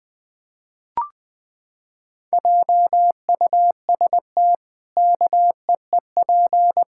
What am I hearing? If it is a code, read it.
Morse code, heard as a keyed tone, about 20 wpm, 700 Hz. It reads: JUST KEEP
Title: Pangram Challenge v2 - 20wpm